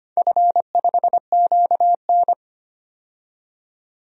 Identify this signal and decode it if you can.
Morse code, heard as a keyed tone, about 25 wpm, 700 Hz. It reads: F5QN
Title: DXCC-100 Call Signs - 4 Characters - 25wpm